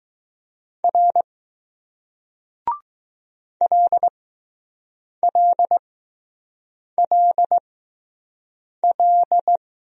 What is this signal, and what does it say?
Morse code, heard as a keyed tone, about 20 wpm, spaced as for 6 wpm, 700 Hz. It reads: R LLLL